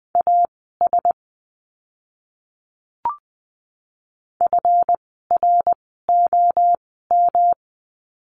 Morse code, heard as a keyed tone, about 20 wpm, spaced as for 14 wpm, 700 Hz. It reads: AS FROM
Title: 237 Lesson 38f - Letter q - Words 20wpm